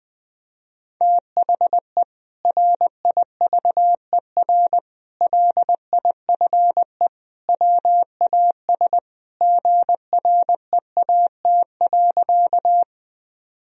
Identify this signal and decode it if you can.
Morse code, heard as a keyed tone, about 20 wpm, 700 Hz. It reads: THE RIVER LIFE WAS GREAT.
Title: Sentences from Top 1000 Words - Review Part 1 - 20wpm